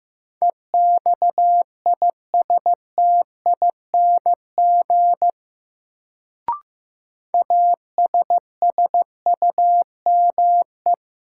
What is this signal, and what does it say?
Morse code, heard as a keyed tone, about 15 wpm, 700 Hz. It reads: EXISTING ASSUME
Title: Sentences from Top 1500 Words - New Word Drill Alt - 15wpm